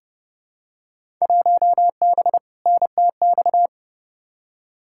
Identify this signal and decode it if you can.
Morse code, heard as a keyed tone, about 30 wpm, 700 Hz. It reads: 1B NTX
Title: Field Day Exchanges - 30wpm